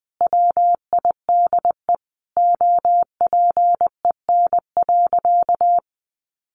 Morse code, heard as a keyed tone, about 20 wpm, 700 Hz. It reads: WIDE OPEN.
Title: Sentences from Top 1500 Words 20wpm